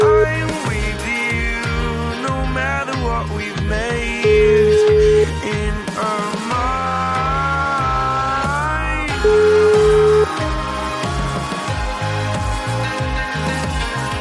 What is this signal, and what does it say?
Ringback tone (Europe, 425 Hz), with noise.